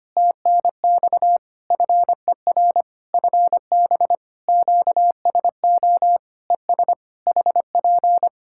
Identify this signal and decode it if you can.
Morse code, heard as a keyed tone, about 25 wpm, 700 Hz. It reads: TNX FER FB QSO ES HP
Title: Practice QSO - Exchange 4 - Ending - 25wpm